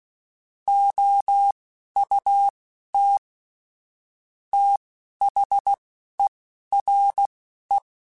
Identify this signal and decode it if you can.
Morse code, heard as a keyed tone, about 16 wpm, spaced as for 12 wpm, 800 Hz. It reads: OUT THERE
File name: QSOs / QSOs-mp3s / WD0BGP 2@10